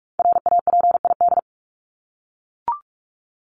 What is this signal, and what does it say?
Morse code, heard as a keyed tone, about 44 wpm, 700 Hz. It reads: RAPID
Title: LY Suffix - Top 100 Words with Base Words 45wpm